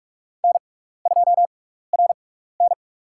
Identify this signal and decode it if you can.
Morse code, heard as a keyed tone, about 46 wpm, spaced as for 16 wpm, 700 Hz. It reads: N2RN
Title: USA Call Signs - 6x Character Spacing - 45wpm